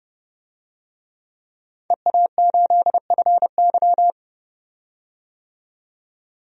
Morse code, heard as a keyed tone, about 30 wpm, 700 Hz. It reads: EA8FY
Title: DXCC-100 Call Signs - 5 Characters - 30wpm